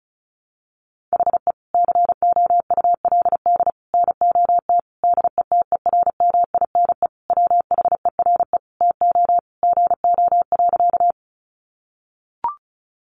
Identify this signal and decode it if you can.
Morse code, heard as a keyed tone, about 34 wpm, 700 Hz. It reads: HE COULD NOT DETERMINE WHERE TO GO.
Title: Sentences from Top 700 Words 35wpm